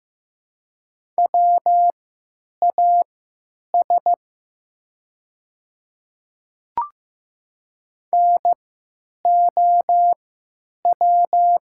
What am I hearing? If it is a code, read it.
Morse code, heard as a keyed tone, about 15 wpm, spaced as for 9 wpm, 700 Hz. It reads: WAS NOW
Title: 3x Character Spacing - Top 100 Words - 15wpm